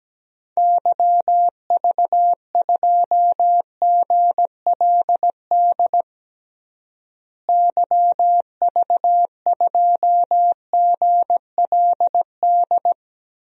Morse code, heard as a keyed tone, about 17 wpm, 700 Hz. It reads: YV2GLD YV2GLD
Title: DXCC-100 Call Signs - 6 Characters - Repeated 2x - 17wpm